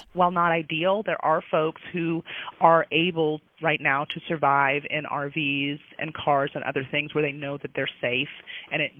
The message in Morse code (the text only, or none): none